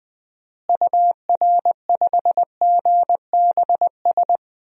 Morse code, heard as a keyed tone, about 21 wpm, 700 Hz. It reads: UR5GBS
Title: DXCC-100 Call Signs - 6 Characters - Repeated 2x - 20wpm